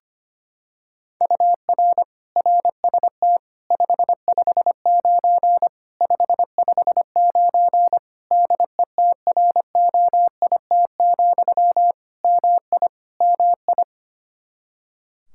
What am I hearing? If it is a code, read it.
Morse code, heard as a keyed tone, about 25 wpm, 700 Hz. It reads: UR RST 559 559 DETROIT, MI MI
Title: Practice QSO - Exchange 1 - Beginning - 25wpm